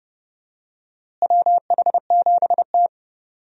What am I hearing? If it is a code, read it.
Morse code, heard as a keyed tone, about 30 wpm, 700 Hz. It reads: WH7T
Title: USA Call Signs - 4 Characters - Repeated 2x - 30wpm